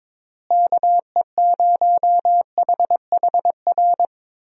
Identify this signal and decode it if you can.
Morse code, heard as a keyed tone, about 22 wpm, 700 Hz. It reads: KE0HHR
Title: USA Call Signs - 6 Characters - Repeated 3x - 22wpm